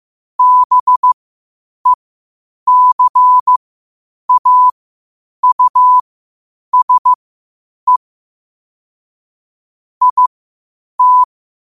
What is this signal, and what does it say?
Morse code, heard as a keyed tone, about 15 wpm, spaced as for 8 wpm, 1000 Hz. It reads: BECAUSE IT
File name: qotdfiles / 8wpm